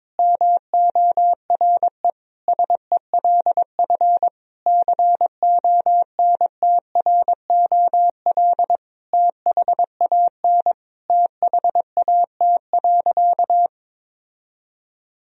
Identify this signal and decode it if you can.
Morse code, heard as a keyed tone, about 22 wpm, 700 Hz. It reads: MORE SELF CONTROL THAN THAT.
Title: Sentences from Top 1000 Words - Review Part 3 - 22wpm